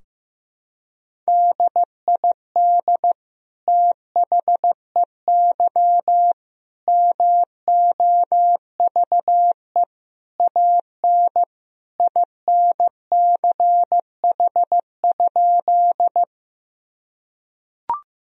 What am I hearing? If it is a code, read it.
Morse code, heard as a keyed tone, about 15 wpm, 700 Hz. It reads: DID THEY MOVE AN INCH?
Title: Sentences from Top 1000 Words - Review Part 3 - 15wpm